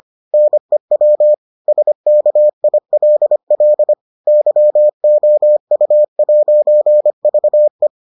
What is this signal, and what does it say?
Morse code, heard as a keyed tone, about 25 wpm, 600 Hz. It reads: NEW SKILL YOU'VE